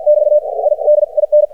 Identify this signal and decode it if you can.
Morse code, heard as a keyed tone, about 53 wpm, 600 Hz. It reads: <BT> I D IN
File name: archived / CRSnet